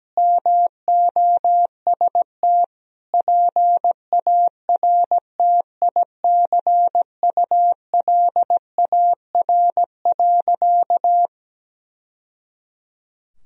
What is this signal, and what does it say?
Morse code, heard as a keyed tone, about 17 wpm, 700 Hz. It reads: MOST PARTICULAR.
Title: Sentences from Top 1000 Words - Review Part 4 - 17wpm